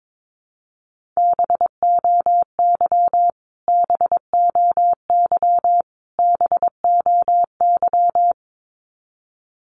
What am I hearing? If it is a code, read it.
Morse code, heard as a keyed tone, about 22 wpm, 700 Hz. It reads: BOY BOY BOY
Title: Top 500 Words Repeated 22wpm